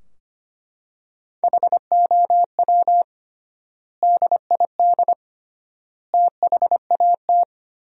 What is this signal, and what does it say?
Morse code, heard as a keyed tone, about 25 wpm, 700 Hz. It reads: HOW DID THAT
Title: 3x Word-Spacing - Sentences from Top 200 Words - 25wpm